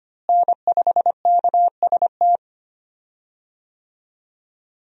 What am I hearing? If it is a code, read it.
Morse code, heard as a keyed tone, about 25 wpm, 700 Hz. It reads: N5KST
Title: USA Call Signs - 5 Characters - Repeated 2x - 25wpm